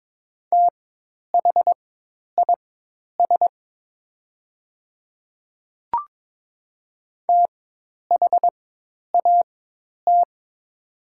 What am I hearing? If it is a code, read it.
Morse code, heard as a keyed tone, about 22 wpm, spaced as for 10 wpm, 700 Hz. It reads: THIS THAT
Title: 4x Character Spacing - Top 100 Words - 22wpm